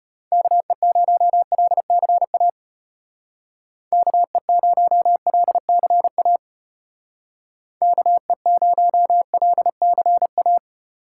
Morse code, heard as a keyed tone, about 33 wpm, 700 Hz. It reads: KE0LCA KE0LCA KE0LCA